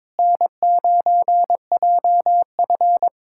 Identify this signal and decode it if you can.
Morse code, heard as a keyed tone, about 22 wpm, 700 Hz. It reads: N9JF
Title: K1USN SST QSO Practice - 22wpm